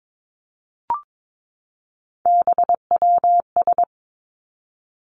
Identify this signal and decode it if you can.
Morse code, heard as a keyed tone, about 22 wpm, 700 Hz. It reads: BWS
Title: Three Challenging Letter-Number Combinations 22wpm